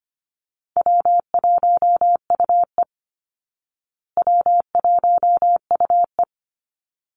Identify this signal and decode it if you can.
Morse code, heard as a keyed tone, about 25 wpm, 700 Hz. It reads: W1UE W1UE
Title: Easy Call Signs Repeated 2x - 25wpm